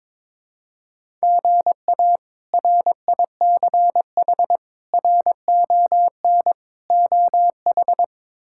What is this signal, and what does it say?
Morse code, heard as a keyed tone, about 22 wpm, 700 Hz. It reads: GA RICH RON OH
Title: K1USN SST QSO Practice - 22wpm